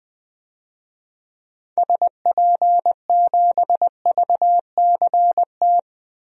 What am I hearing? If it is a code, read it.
Morse code, heard as a keyed tone, about 20 wpm, 700 Hz. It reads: SP7VCT
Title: DXCC-100 Call Signs - 6 Characters - 20wpm